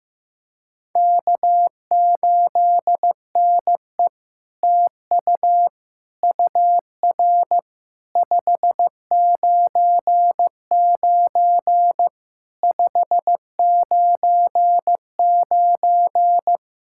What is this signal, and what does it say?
Morse code, heard as a keyed tone, about 15 wpm, 700 Hz. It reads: K8NE TU UR 599 599
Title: SOTA QSO Practice 15wpm